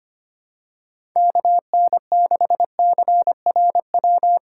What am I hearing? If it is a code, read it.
Morse code, heard as a keyed tone, about 25 wpm, 700 Hz. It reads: KN6CRW
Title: USA Call Signs - 6 Characters - 3x Character Spacing - 25wpm